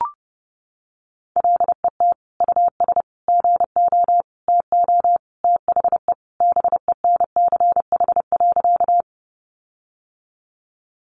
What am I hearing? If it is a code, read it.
Morse code, heard as a keyed tone, about 30 wpm, 700 Hz. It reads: LET US GO TO THE BENCH.
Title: Sentences from Top 1400 Words 30wpm